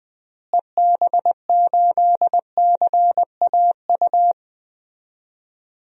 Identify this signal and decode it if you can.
Morse code, heard as a keyed tone, about 21 wpm, 700 Hz. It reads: EB8CAU
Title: DXCC-100 Call Signs - 6 Characters - Repeated 3x - 20wpm